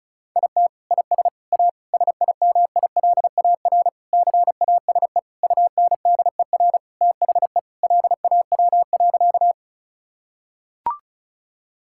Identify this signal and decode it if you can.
Morse code, heard as a keyed tone, about 36 wpm, 700 Hz. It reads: IT IS A SIMILAR CASE UNDER THE LAW.
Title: Sentences from Top 1000 Words - Review Part 5 - 35wpm